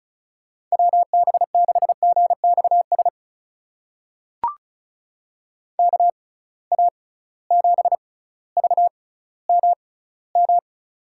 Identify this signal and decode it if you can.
Morse code, heard as a keyed tone, about 36 wpm, spaced as for 18 wpm, 700 Hz. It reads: WB6GXS KA7VMM
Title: USA Call Signs - 6 Characters - 6x Character Spacing - 35wpm